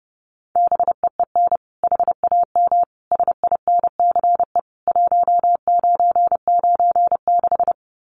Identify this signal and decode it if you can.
Morse code, heard as a keyed tone, about 28 wpm, 700 Hz. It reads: BEEN HAM SINCE 1996
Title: Common QSO Phrases 30wpm